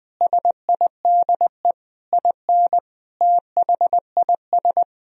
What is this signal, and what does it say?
Morse code, heard as a keyed tone, about 20 wpm, 700 Hz. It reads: SIDE IN THIS
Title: Sentences from Top 1000 Words - Review Part 1 - 20wpm